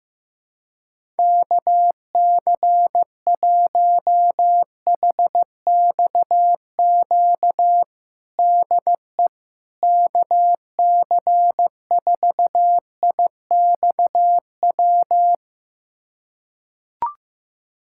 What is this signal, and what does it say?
Morse code, heard as a keyed tone, about 15 wpm, 700 Hz. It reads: KC1HXQ DE KC4IXW
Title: USA Callsign DE Callsign - 6 Characters - 15wpm